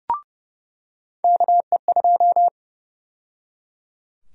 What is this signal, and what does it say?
Morse code, heard as a keyed tone, about 29 wpm, 700 Hz. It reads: KE2
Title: USA Call Sign Prefixes - 30wpm